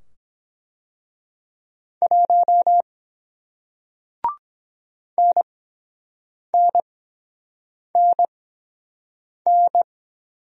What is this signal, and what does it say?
Morse code, heard as a keyed tone, about 23 wpm, spaced as for 7 wpm, 700 Hz. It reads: 1 NNNN